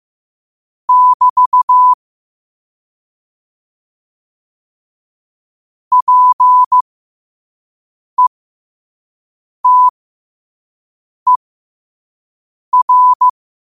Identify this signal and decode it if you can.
Morse code, heard as a keyed tone, about 15 wpm, spaced as for 5 wpm, 1000 Hz. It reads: <BT> PETER
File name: qotdfiles / 5wpm